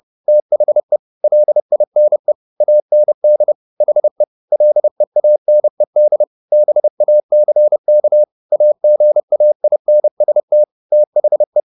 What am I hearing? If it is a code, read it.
Morse code, heard as a keyed tone, about 30 wpm, 600 Hz. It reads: THE LINE AND HE LEANED BACK AGAINST THE